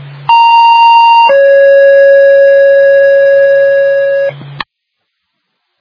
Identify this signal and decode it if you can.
Two-tone page, heard as a one-second tone, then a three-second tone, about 930 then 550 Hz. It is heard from 0.3 to 4.3 s.